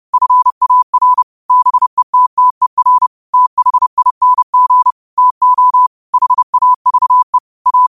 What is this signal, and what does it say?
Morse code, heard as a keyed tone, about 30 wpm, 1000 Hz. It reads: FAR BETTER THING TO HAVE A